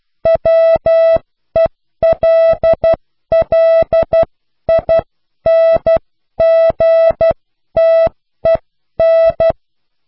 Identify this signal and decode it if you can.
Morse code, heard as a keyed tone, about 12 wpm, spaced as for 10 wpm, 660 Hz. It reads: WELLINGTEN
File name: cw / pseudo worden(10)